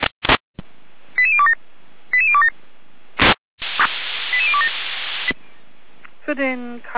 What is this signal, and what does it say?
Five-tone selective call (ZVEI), heard as quick runs of five tones, tones of about 70 ms, with speech. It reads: 88027 88027 88027